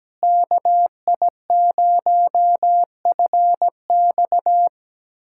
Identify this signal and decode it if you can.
Morse code, heard as a keyed tone, about 17 wpm, 700 Hz. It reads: KI0FX
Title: USA Callsign DE Callsign 17wpm